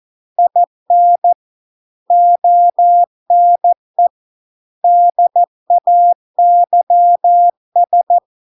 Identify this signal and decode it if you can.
Morse code, heard as a keyed tone, about 14 wpm, 700 Hz. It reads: IN ONE DAYS